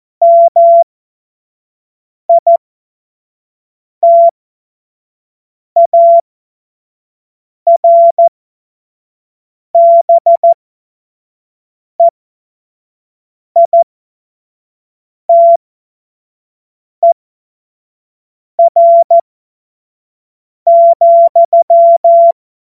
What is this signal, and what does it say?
Morse code, heard as a keyed tone, about 14 wpm, spaced as for 5 wpm, 680 Hz. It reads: MITARBEITER,
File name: Morsekurs / Witze / Witz 03 Juni 2025 25 F70